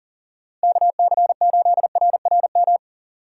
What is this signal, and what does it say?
Morse code, heard as a keyed tone, about 37 wpm, 700 Hz. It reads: KC8RRM
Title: USA Call Signs - 6 Characters - 6x Character Spacing - 40wpm